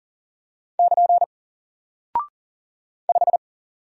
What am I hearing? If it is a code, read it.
Morse code, heard as a keyed tone, about 40 wpm, 700 Hz. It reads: <KN> 5
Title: Single Character (Letter:Number:Punctuation:Prosign) Alt 40wpm